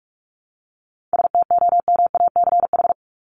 Morse code, heard as a keyed tone, about 44 wpm, 700 Hz. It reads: STOMACH
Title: Top 1501-1750 Words 45wpm